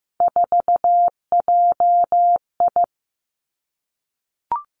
Morse code, heard as a keyed tone, about 15 wpm, 700 Hz. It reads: 4JI
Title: Three Challenging Letter-Number Combinations 15wpm